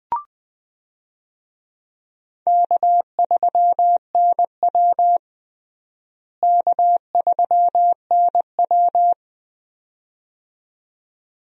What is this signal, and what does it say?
Morse code, heard as a keyed tone, about 21 wpm, 700 Hz. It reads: K3NW K3NW
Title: USA Call Signs - 4 Characters - Repeated 2x - 20wpm